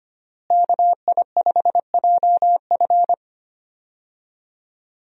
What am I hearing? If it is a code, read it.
Morse code, heard as a keyed tone, about 25 wpm, 700 Hz. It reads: KI5JF
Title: USA Call Signs - 5 Characters - Repeated 3x - 25wpm